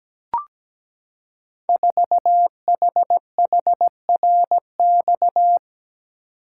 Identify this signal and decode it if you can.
Morse code, heard as a keyed tone, about 17 wpm, 700 Hz. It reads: 4HHRX
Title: Code Groups - 5 Letter:Numbers - 17wpm